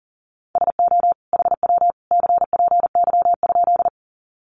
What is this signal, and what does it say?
Morse code, heard as a keyed tone, about 40 wpm, 700 Hz. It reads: SO HW CPY?
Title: Common QSO Phrases 40wpm